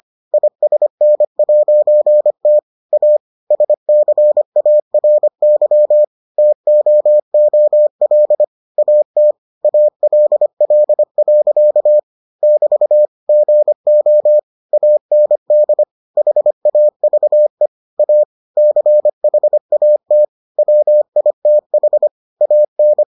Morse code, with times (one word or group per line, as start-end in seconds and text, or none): 0.3-2.6 ISN'T
2.9-3.2 A
3.5-6.0 SCARY
6.4-8.4 TOOL
8.8-9.3 AT
9.6-12.0 ALL.
12.4-13.1 <BT>
13.3-14.4 GO
14.7-15.8 AND
16.2-17.7 HAVE
18.0-18.2 A
18.6-20.2 CHAT
20.6-22.1 WITH
22.4-23.0 AN